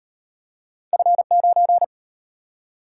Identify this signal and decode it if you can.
Morse code, heard as a keyed tone, about 33 wpm, 700 Hz. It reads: F9